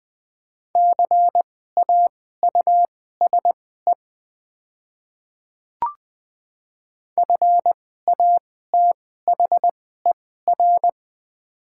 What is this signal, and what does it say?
Morse code, heard as a keyed tone, about 20 wpm, spaced as for 14 wpm, 700 Hz. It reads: CAUSE FATHER
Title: 2x Character Spacing - Top 200 Words - 20wpm